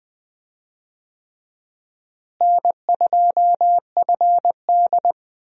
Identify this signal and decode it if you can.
Morse code, heard as a keyed tone, about 21 wpm, 700 Hz. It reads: N2FD
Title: USA Call Signs - 4 Characters - Repeated 3x - 20wpm